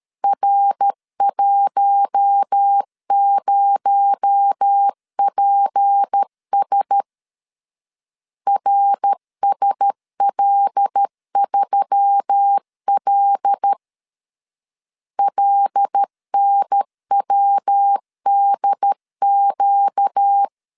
Morse code, heard as a keyed tone, about 13 wpm, 800 Hz. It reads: R10PS RSL3L LNWDQ
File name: CW / DET / 1 6 1500C S A 12wpm 3440